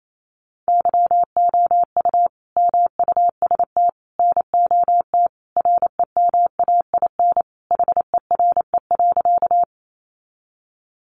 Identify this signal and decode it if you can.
Morse code, heard as a keyed tone, about 28 wpm, 700 Hz. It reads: YOU MUST NOT REMAIN HERE.
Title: Sentences from Top 1200 Words 28wpm